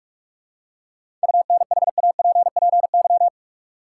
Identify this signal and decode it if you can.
Morse code, heard as a keyed tone, about 46 wpm, 700 Hz. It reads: UNHAPPY
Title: Seven Letter Words - 45wpm